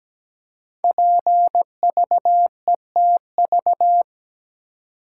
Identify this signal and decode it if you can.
Morse code, heard as a keyed tone, about 17 wpm, 700 Hz. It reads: PVETV
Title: Code Groups - 5 Letters - 17wpm